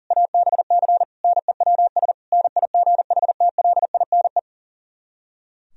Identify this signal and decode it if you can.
Morse code, heard as a keyed tone, about 51 wpm, 700 Hz. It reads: ABC NEWS NIGHTLINE